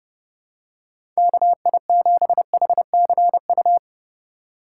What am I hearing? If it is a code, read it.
Morse code, heard as a keyed tone, about 31 wpm, 700 Hz. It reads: KI7HCU
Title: USA Call Signs - 6 Characters - 2x Character Spacing - 30wpm